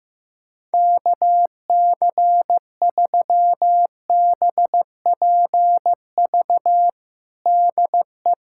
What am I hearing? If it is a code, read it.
Morse code, heard as a keyed tone, about 15 wpm, 700 Hz. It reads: KC3BPV DE
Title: USA Callsign DE Callsign 15wpm